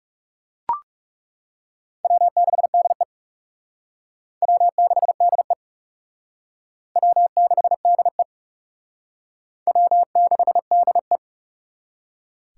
Morse code, heard as a keyed tone, about 40 wpm, 700 Hz. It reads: W6DE W6DE W6DE W6DE